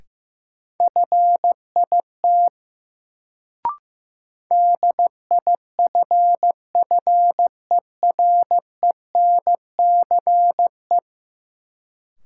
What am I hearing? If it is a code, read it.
Morse code, heard as a keyed tone, about 15 wpm, 700 Hz. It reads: FIT DIFFERENCE
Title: Sentences from Top 800 Words - New Word Drill - 15wpm